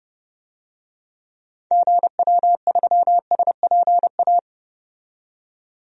Morse code, heard as a keyed tone, about 30 wpm, 700 Hz. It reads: GW3SPA
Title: DXCC-100 Call Signs - 6 Characters - 30wpm